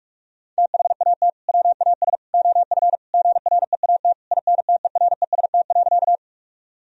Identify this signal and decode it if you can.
Morse code, heard as a keyed tone, about 41 wpm, 700 Hz. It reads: THAT WAS OF GREAT INTEREST.
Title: Sentences from Top 1000 Words - Review Part 2 - 45wpm